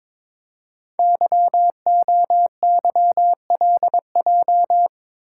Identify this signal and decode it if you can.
Morse code, heard as a keyed tone, about 22 wpm, 700 Hz. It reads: YOYLJ
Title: Code Groups - 5 Letter:Numbers - 22wpm